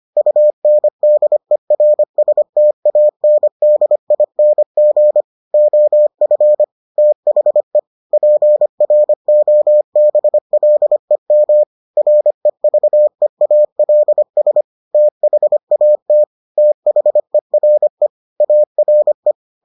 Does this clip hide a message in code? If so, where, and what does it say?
Yes: Morse code from 0.2 to 19.3 s, heard as a keyed tone, about 25 wpm, 600 Hz. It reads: UNDERSTANDING OF THE PROBLEM REVEALS THAT THERE ARE